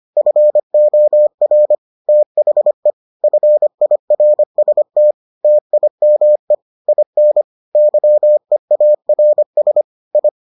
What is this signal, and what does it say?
Morse code, heard as a keyed tone, about 25 wpm, 600 Hz. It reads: FOR THE FIRST TIME IN YEARS I